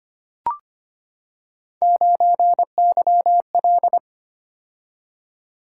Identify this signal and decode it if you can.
Morse code, heard as a keyed tone, about 25 wpm, 700 Hz. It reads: 9YL
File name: Three Letter-Number Combinations - 25wpm